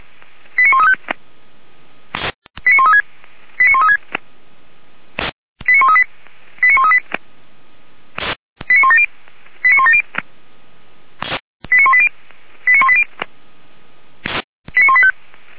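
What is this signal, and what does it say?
Five-tone selective call (ZVEI), heard as quick runs of five tones, tones of about 80 ms. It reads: 89135 89136 89136 89138 89138 89170 89170 89180 89180 89185
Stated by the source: ZVEI five-tone selective call received on an SDR, channel 464.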